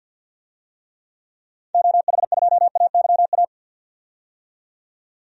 Morse code, heard as a keyed tone, about 51 wpm, 700 Hz. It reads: OH2AYA
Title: DXCC-100 Call Signs - 6 Characters - 50wpm